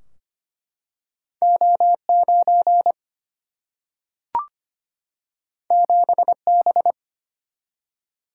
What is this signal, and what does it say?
Morse code, heard as a keyed tone, about 25 wpm, 700 Hz. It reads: O 9 7 B
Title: Single-Letter Single-Number 25wpm